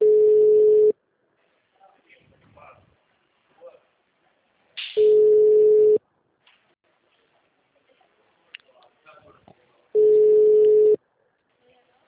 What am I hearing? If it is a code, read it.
Ringback tone (Europe, 425 Hz), with speech.